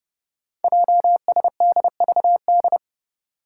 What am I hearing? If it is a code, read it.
Morse code, heard as a keyed tone, about 30 wpm, 700 Hz. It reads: JSDVD
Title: Code Groups - 5 Letters - 30wpm